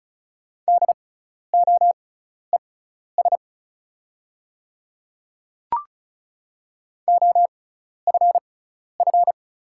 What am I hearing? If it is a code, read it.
Morse code, heard as a keyed tone, about 34 wpm, spaced as for 12 wpm, 690 Hz. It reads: DOES OFF
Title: 6x Character Spacing - Top 200 Words - 35wpm